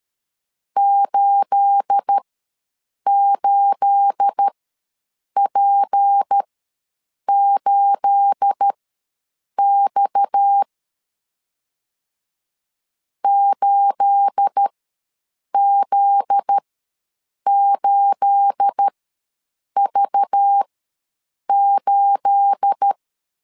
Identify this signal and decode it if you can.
Morse code, heard as a keyed tone, about 13 wpm, spaced as for 7 wpm, 800 Hz. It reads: 88P8X 8Z8V8
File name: LEZIONI / 9 28 ws 12wpm 3354